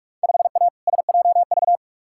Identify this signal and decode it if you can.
Morse code, heard as a keyed tone, about 46 wpm, 700 Hz. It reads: 5A SJV
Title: Field Day Exchanges - 45wpm